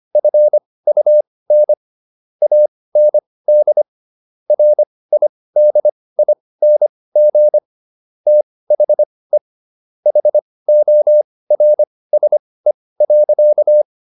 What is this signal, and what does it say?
Morse code, heard as a keyed tone, about 25 wpm, spaced as for 18 wpm, 600 Hz. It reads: FUN AND RIDING THE HORSE.